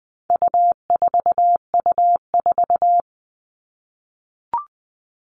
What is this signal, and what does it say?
Morse code, heard as a keyed tone, about 20 wpm, 700 Hz. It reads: U4U4
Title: All Cut Numbers - 4-Digit 20wpm